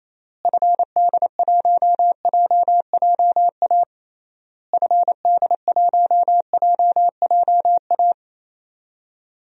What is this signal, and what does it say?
Morse code, heard as a keyed tone, about 28 wpm, 700 Hz. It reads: FD1JJA FD1JJA
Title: DXCC-100 Call Signs - 6 Characters - Repeated 3x - 28wpm